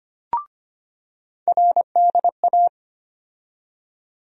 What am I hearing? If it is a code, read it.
Morse code, heard as a keyed tone, about 25 wpm, 700 Hz. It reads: RDA
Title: USA Call Sign Postfixes - 3 Letters 25wpm